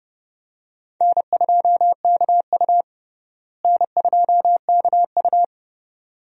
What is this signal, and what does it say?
Morse code, heard as a keyed tone, about 32 wpm, 700 Hz. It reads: N2KU N2KU
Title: USA Call Signs - 4 Characters - Repeated 2x - 30wpm